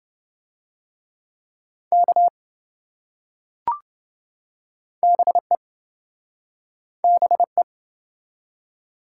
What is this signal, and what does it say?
Morse code, heard as a keyed tone, about 26 wpm, 700 Hz. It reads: K BE BE